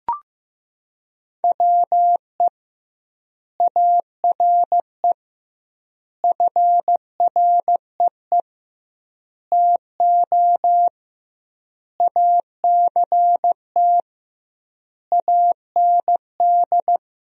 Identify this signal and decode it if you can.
Morse code, heard as a keyed tone, about 15 wpm, 700 Hz. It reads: WE ARE FREE TO ACT AND
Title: 2x Word-Spacing - Sentences from Top 500 Words - 15wpm